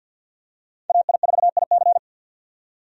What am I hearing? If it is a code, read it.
Morse code, heard as a keyed tone, about 50 wpm, 700 Hz. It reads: AI4IC
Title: USA Call Signs - 5 Characters - 50wpm